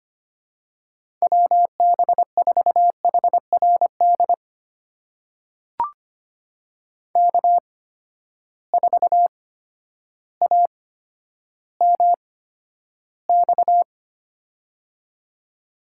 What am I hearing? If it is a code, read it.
Morse code, heard as a keyed tone, about 25 wpm, spaced as for 11 wpm, 700 Hz. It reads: WB4HRD K4AMX
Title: USA Call Signs - 8x Character Spacing - 25wpm